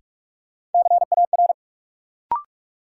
Morse code, heard as a keyed tone, about 41 wpm, 700 Hz. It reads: CAR
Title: Sets of 5 Words Encore - New Word Drill - 45wpm